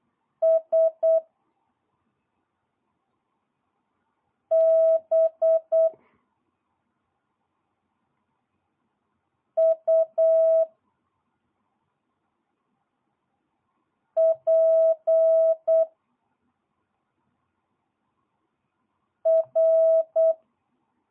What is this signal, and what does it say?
Morse code, heard as a keyed tone, about 8 wpm, 660 Hz. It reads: S B U P R
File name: jh8cbh / 2022cw